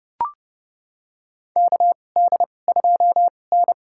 Morse code, heard as a keyed tone, about 30 wpm, spaced as for 22 wpm, 700 Hz. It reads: KD2N
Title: USA Call Signs - 4 Characters - 2x Character Spacing - 30wpm